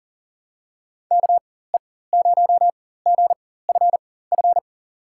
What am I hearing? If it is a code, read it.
Morse code, heard as a keyed tone, about 38 wpm, spaced as for 18 wpm, 700 Hz. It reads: KE0GFF
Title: USA Call Signs - 6 Characters - 4x Character Spacing - 40wpm